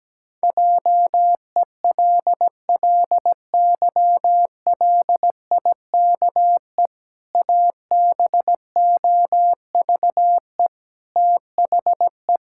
Morse code, heard as a keyed tone, about 17 wpm, 700 Hz. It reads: JELLYLIKE ABOVE THE
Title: Pangram Challenge v2 - 17wpm